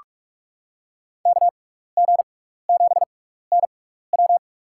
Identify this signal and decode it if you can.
Morse code, heard as a keyed tone, about 46 wpm, spaced as for 15 wpm, 700 Hz. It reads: KG7NW
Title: USA Call Signs - 5 Characters - 6x Character Spacing - 45wpm